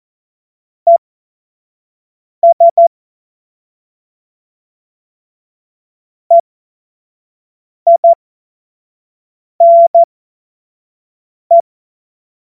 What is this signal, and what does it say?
Morse code, heard as a keyed tone, about 14 wpm, spaced as for 5 wpm, 680 Hz. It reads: ES EINE
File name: Morsekurs / Witze / Witz 25. April 2023 25 F70